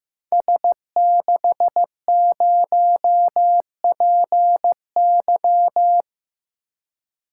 Morse code, heard as a keyed tone, about 15 wpm, 700 Hz. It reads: S60PY
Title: Code Groups - 5 Letter:Numbers - 15wpm